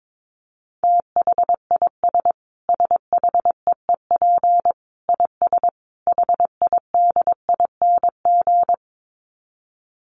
Morse code, heard as a keyed tone, about 22 wpm, 700 Hz. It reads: THIS SHEEP IS HIDING
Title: Dit Challenge 22wpm